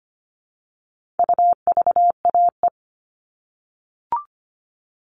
Morse code, heard as a keyed tone, about 25 wpm, 700 Hz. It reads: UVAE
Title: All Cut Numbers - 4-Digit 25wpm